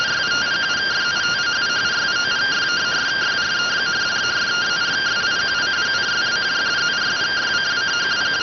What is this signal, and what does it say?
Two-tone teleprinter signal, about 170 Hz shift.